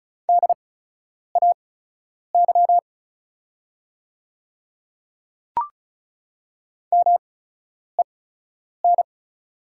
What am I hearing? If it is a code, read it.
Morse code, heard as a keyed tone, about 36 wpm, spaced as for 10 wpm, 700 Hz. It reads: DAY MEN
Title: 8x Character Spacing - Top 300 Words - 35wpm